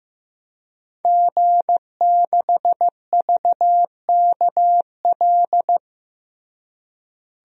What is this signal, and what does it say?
Morse code, heard as a keyed tone, about 16 wpm, 700 Hz. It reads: G6VKL
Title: DXCC-100 Call Signs - 5 Characters - Repeated 3x - 15wpm